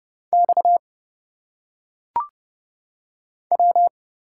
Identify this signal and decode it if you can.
Morse code, heard as a keyed tone, about 30 wpm, 700 Hz. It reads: X W
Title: Single Letters - Phonetic Pronunciation 30wpm